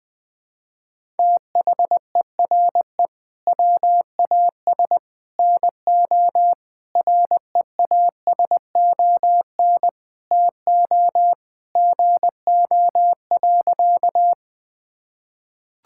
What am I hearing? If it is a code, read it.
Morse code, heard as a keyed tone, about 20 wpm, 700 Hz. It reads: THERE WAS NO REASON TO GO.
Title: Sentences with 3-Word Phrases - Spoken Beginning - 20wpm